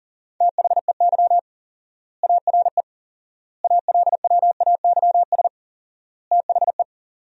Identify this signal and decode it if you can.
Morse code, heard as a keyed tone, about 40 wpm, 700 Hz. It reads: THEY ARE ALWAYS THE
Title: 4x Word-Spacing - Sentences from Top 300 Words - 40wpm